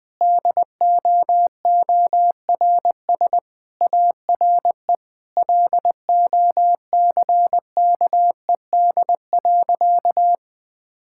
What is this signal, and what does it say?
Morse code, heard as a keyed tone, about 20 wpm, 700 Hz. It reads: DOORS ARE LOCKED.
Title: Sentences with 5-Word Phrases - Spoken Beginning - 20wpm